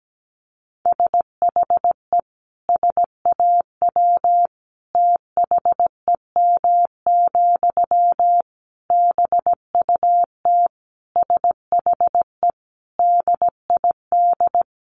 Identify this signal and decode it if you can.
Morse code, heard as a keyed tone, about 17 wpm, 700 Hz. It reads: SHE SAW THEM, BUT SHE DID
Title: Sentences from Top 1000 Words 17wpm